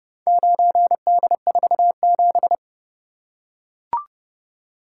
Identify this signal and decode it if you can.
Morse code, heard as a keyed tone, about 29 wpm, 700 Hz. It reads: 9D47